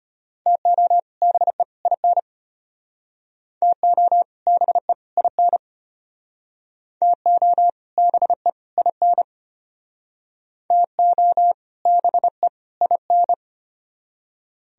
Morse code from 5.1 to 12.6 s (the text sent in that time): IN TO BE IN TO BE